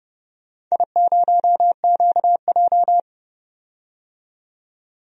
Morse code, heard as a keyed tone, about 32 wpm, 700 Hz. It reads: I0QJ
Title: DXCC-100 Call Signs - 4 Characters - 30wpm